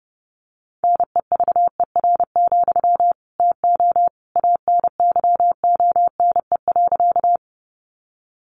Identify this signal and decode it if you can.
Morse code, heard as a keyed tone, about 28 wpm, 700 Hz. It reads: NEVER, TO ANYONE.
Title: Sentences from Top 900 Words 30wpm